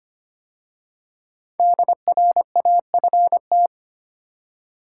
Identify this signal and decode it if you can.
Morse code, heard as a keyed tone, about 25 wpm, 700 Hz. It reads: DRAFT
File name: 5 Letter Words - First 3 Letters Given - 25wpm